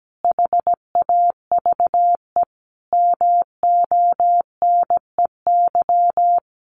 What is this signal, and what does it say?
Morse code, heard as a keyed tone, about 17 wpm, 700 Hz. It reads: HAVE MONEY
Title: Sentences from Top 1400 Words 17wpm